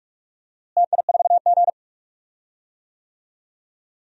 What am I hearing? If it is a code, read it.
Morse code, heard as a keyed tone, about 44 wpm, 700 Hz. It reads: TI4G
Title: DXCC-100 Call Signs - 4 Characters - Repeated 3x - 45wpm